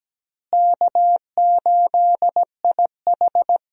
Morse code, heard as a keyed tone, about 17 wpm, 700 Hz. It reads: K8IH
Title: USA Call Signs - 4 Characters - Repeated 2x - 17wpm